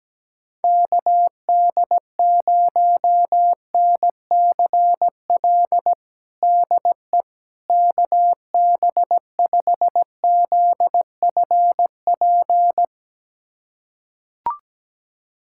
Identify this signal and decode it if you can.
Morse code, heard as a keyed tone, about 17 wpm, 700 Hz. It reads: KD0NCL DE KB5ZFP
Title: USA Callsign DE Callsign - 6 Characters - 17wpm